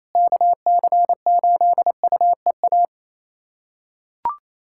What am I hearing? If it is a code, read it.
Morse code, heard as a keyed tone, about 28 wpm, 700 Hz. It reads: KC8UEA
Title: USA Callsign DE Callsign - 6 Characters - 28wpm